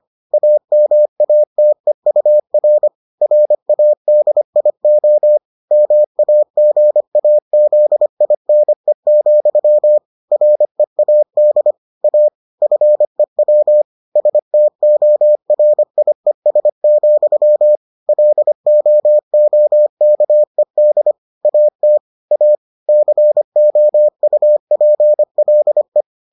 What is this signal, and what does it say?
Morse code, heard as a keyed tone, about 25 wpm, 600 Hz. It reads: AMATEUR RADIO MAGAZINE, READ A FEW STORIES, LOOKED AT A COUPLE